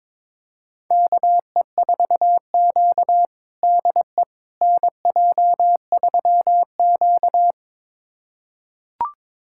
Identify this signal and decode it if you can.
Morse code, heard as a keyed tone, about 22 wpm, 700 Hz. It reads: KE4Q DE NJ3Q
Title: USA Callsign DE Callsign - 4 Characters - 22wpm